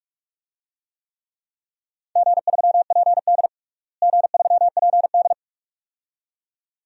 Morse code, heard as a keyed tone, about 41 wpm, 700 Hz. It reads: G3PD G3PD
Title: DXCC-100 Call Signs - 4 Characters - Repeated 2x - 45wpm